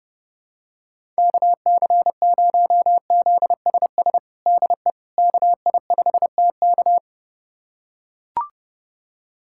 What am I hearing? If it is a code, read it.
Morse code, heard as a keyed tone, about 32 wpm, 700 Hz. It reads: KC0ZSS DE KI5TK
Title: USA Callsign DE Callsign 30wpm